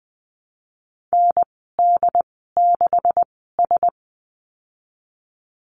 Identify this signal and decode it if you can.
Morse code, heard as a keyed tone, about 20 wpm, spaced as for 15 wpm, 700 Hz. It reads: ND6S
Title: 147 Lesson 24g - Letter y - Call Signs 20wpm